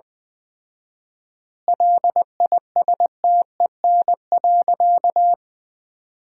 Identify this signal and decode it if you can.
Morse code, heard as a keyed tone, about 21 wpm, 700 Hz. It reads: LISTEN.
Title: 4x Word-Spacing - Sentences from Top 400 Words - 20wpm